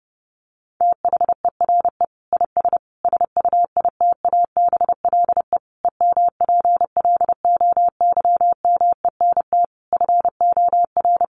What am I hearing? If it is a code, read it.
Morse code, heard as a keyed tone, about 30 wpm, 700 Hz. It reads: THERE IS SUITABLE EMPLOYMENT FOR
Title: Sentences from Top 1500 Words 30wpm